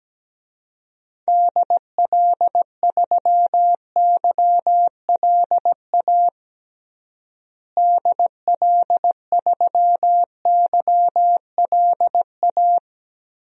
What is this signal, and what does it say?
Morse code, heard as a keyed tone, about 17 wpm, 700 Hz. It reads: DL3YLA DL3YLA
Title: DXCC-100 Call Signs - 6 Characters - Repeated 2x - 17wpm